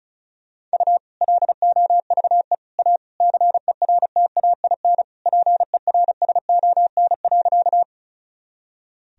Morse code, heard as a keyed tone, about 36 wpm, 700 Hz. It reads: U LOVE A CERTAIN PERSON.
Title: Sentences from Top 500 Words and Abbreviations 35wpm